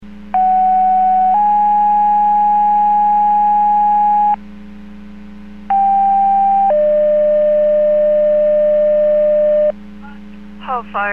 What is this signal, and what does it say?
Two-tone page, heard as a one-second tone, then a three-second tone, about 750 then 830 Hz, about 790 then 600 Hz, with speech. It is heard from 0.3 to 9.7 s.